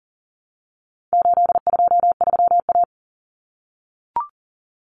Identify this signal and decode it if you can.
Morse code, heard as a keyed tone, about 40 wpm, 700 Hz. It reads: 823A
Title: Common Cut Numbers - 4-Digit 40wpm